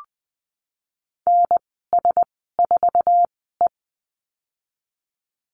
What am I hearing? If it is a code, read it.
Morse code, heard as a keyed tone, about 20 wpm, spaced as for 15 wpm, 700 Hz. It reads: NS4E
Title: 062 Lesson 11g - Letter h - Call Signs 20wpm